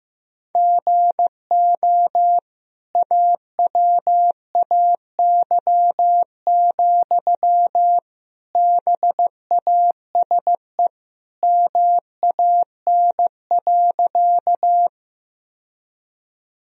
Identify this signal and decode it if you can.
Morse code, heard as a keyed tone, about 15 wpm, 700 Hz. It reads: GO AWAY, BASE MAN.
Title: Sentences from Top 1000 Words - Review Part 1 - 15wpm